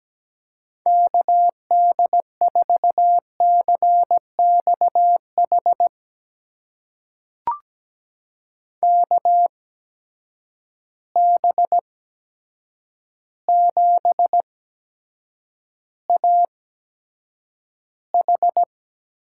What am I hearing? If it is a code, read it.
Morse code, heard as a keyed tone, about 17 wpm, spaced as for 8 wpm, 700 Hz. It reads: KD4CXH KB7AH
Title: USA Call Signs - 6 Characters - 8x Character Spacing - 17wpm